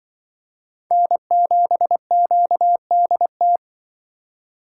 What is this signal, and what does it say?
Morse code, heard as a keyed tone, about 24 wpm, 700 Hz. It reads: N7QDT
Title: Speed Racing - USA 1x3 Call Signs 20wpm